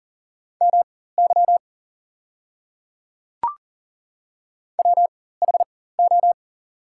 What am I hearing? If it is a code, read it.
Morse code, heard as a keyed tone, about 40 wpm, spaced as for 19 wpm, 700 Hz. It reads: MY WHO
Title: 4x Character Spacing - Top 100 Words - 40wpm